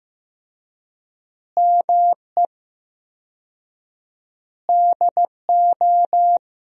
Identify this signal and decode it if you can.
Morse code, heard as a keyed tone, about 16 wpm, 700 Hz. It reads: ME DO
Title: 4x Word-Spacing - Sentences from Top 300 Words - 15wpm